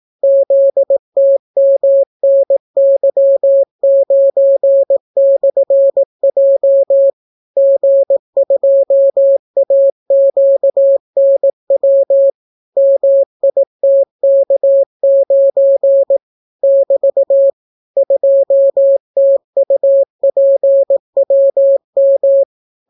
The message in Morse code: ZTMNY9/J G2AQNW MITK9 <BT> 2TUPWM

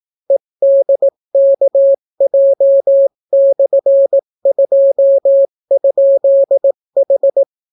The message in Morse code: EDKJ/2?H

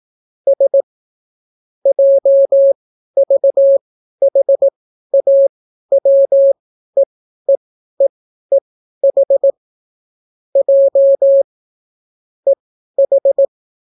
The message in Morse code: S JVHAWEEEEH J EH

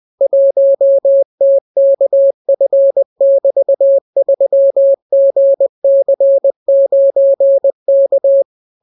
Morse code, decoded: 1TKF<BT>3GC9K

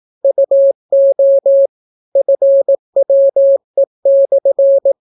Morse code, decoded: UO FWE/

